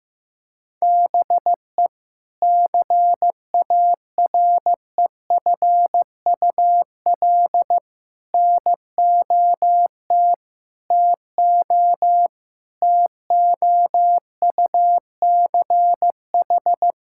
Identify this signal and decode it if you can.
Morse code, heard as a keyed tone, about 15 wpm, 700 Hz. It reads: BE CAREFUL NOT TO TOUCH
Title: Sentences from Top 1600 Words 15wpm